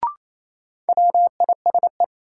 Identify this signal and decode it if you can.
Morse code, heard as a keyed tone, about 28 wpm, 700 Hz. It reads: WISE
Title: Sentences from Top 1600 Words - New Word Drill Alt - 28wpm